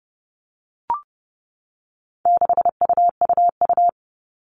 Morse code, heard as a keyed tone, about 30 wpm, 700 Hz. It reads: 6UUU
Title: All Cut Numbers - 4-Digit 30wpm